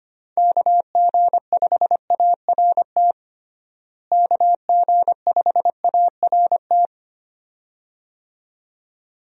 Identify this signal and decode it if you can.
Morse code, heard as a keyed tone, about 25 wpm, 700 Hz. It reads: KG5ART KG5ART
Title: USA Call Signs - 6 Characters - Repeated 2x - 25wpm